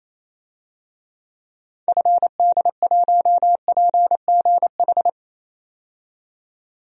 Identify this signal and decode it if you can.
Morse code, heard as a keyed tone, about 28 wpm, 700 Hz. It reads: FD1PGH
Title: DXCC-100 Call Signs - 6 Characters - 28wpm